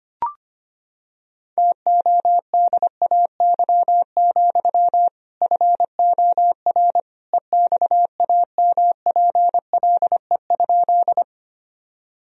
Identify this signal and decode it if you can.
Morse code, heard as a keyed tone, about 25 wpm, 700 Hz. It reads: TODAY, FOR EXAMPLE?
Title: Sentences from Top 1000 Words - Review Part 1 - 25wpm